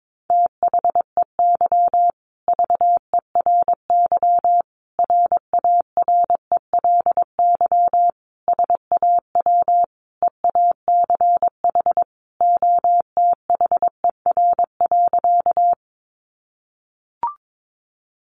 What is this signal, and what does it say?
Morse code, heard as a keyed tone, about 22 wpm, 700 Hz. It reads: THEY VERY RARELY SAW EACH OTHER.
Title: Sentences from Top 1400 Words 22wpm